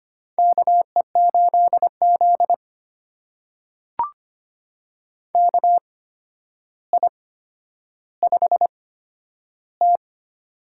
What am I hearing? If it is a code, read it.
Morse code, heard as a keyed tone, about 25 wpm, spaced as for 11 wpm, 700 Hz. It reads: KE8Z KI5T
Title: USA Call Signs - 8x Character Spacing - 25wpm